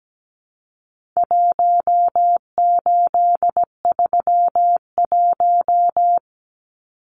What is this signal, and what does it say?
Morse code, heard as a keyed tone, about 17 wpm, 700 Hz. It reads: 1831
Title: Numbers - Contest Style - 4-Digit 17wpm